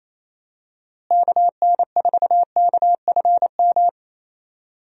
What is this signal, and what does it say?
Morse code, heard as a keyed tone, about 28 wpm, 700 Hz. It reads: KN4KFM
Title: USA Call Signs - 6 Characters - Repeated 3x - 28wpm